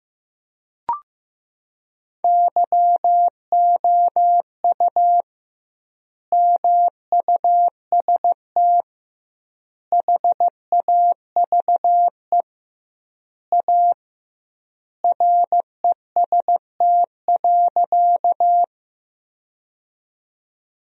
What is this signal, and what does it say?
Morse code, heard as a keyed tone, about 16 wpm, 700 Hz. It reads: YOU MUST HAVE A REST.
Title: 2x Word-Spacing - Sentences from Top 500 Words - 15wpm